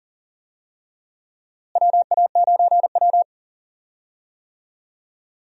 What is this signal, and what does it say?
Morse code, heard as a keyed tone, about 40 wpm, 700 Hz. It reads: WA9W
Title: USA Call Signs - 4 Characters - Repeated 3x - 40wpm